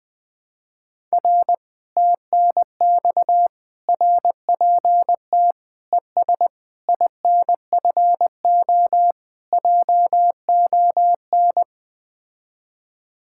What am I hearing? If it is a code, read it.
Morse code, heard as a keyed tone, about 20 wpm, 700 Hz. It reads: R TNX RPT ES INFO JON
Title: Speed-Racing Common QSO Phrases - 17wpm